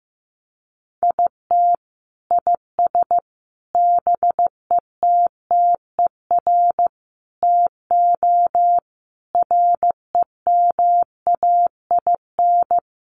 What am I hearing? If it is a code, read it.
Morse code, heard as a keyed tone, about 15 wpm, 700 Hz. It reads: IT IS BETTER TO REMAIN
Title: Sentences from Top 1200 Words 15wpm